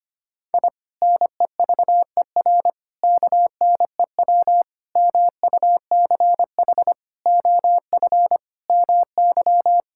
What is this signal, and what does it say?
Morse code, heard as a keyed tone, about 25 wpm, 700 Hz. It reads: I NEVER KNEW MUCH OF MY
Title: Sentences from Top 1000 Words - Review Part 6 - 25wpm